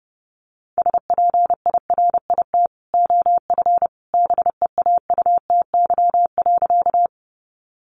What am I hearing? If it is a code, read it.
Morse code, heard as a keyed tone, about 30 wpm, 700 Hz. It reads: SPIRIT OF BEAUTY.
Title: Sentences from Top 1400 Words 30wpm